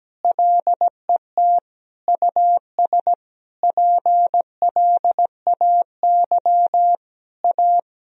Morse code, heard as a keyed tone, about 17 wpm, 700 Hz. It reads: LET US PLAY A
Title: Sentences from Top 1000 Words - Review Part 4 - 17wpm